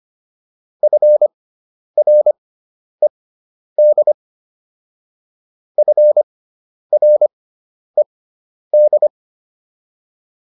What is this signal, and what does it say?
Morse code, heard as a keyed tone, about 25 wpm, spaced as for 10 wpm, 600 Hz. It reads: FRED FRED
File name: QSO404 10